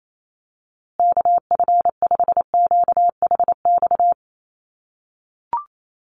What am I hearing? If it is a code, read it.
Morse code, heard as a keyed tone, about 28 wpm, 700 Hz. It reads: KF5QHX
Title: USA Call Signs Repeated 2x 28wpm